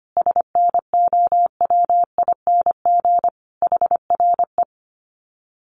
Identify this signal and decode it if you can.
Morse code, heard as a keyed tone, about 25 wpm, 700 Hz. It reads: SNOWING HRE
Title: Common QSO Phrases 25wpm